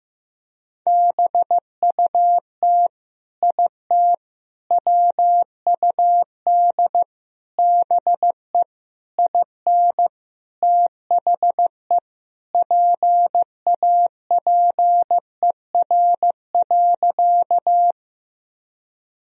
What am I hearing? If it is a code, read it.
Morse code, heard as a keyed tone, about 15 wpm, 700 Hz. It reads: BUT IT WUD BE IN THE PAPER.
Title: Sentences from Top 300 Words and Abbreviations 15wpm